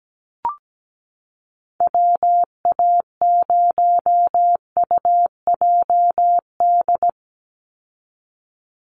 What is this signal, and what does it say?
Morse code, heard as a keyed tone, about 17 wpm, 700 Hz. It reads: WA0UJD
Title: USA Call Signs 17wpm